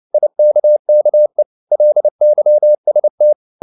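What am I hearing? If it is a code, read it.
Morse code, heard as a keyed tone, about 29 wpm, 600 Hz. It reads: IKKE LYST
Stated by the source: Fast (QRQ) Morse at 29 wpm, from CW med Gnister.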